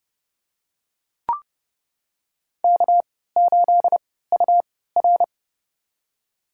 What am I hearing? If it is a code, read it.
Morse code, heard as a keyed tone, about 32 wpm, spaced as for 17 wpm, 700 Hz. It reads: K8UR
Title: USA Call Signs - 3x Character Spacing - 30wpm